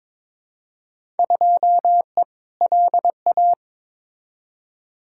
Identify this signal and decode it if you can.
Morse code, heard as a keyed tone, about 22 wpm, 700 Hz. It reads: 2E LA
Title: Field Day Exchanges Encore - 22wpm